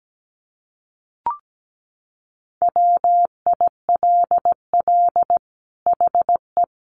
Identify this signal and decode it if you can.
Morse code, heard as a keyed tone, about 17 wpm, 700 Hz. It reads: WILL HE
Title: Sentences from Top 1400 Words 17wpm